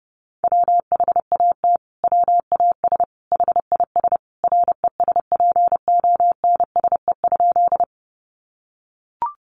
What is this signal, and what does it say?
Morse code, heard as a keyed tone, about 30 wpm, 700 Hz. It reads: WHAT WAS HIS RESPONSE?
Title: Sentences from Top 1400 Words 30wpm